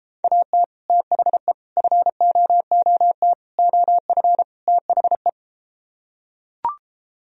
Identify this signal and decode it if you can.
Morse code, heard as a keyed tone, about 34 wpm, 700 Hz. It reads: AT THE FOOT OF THE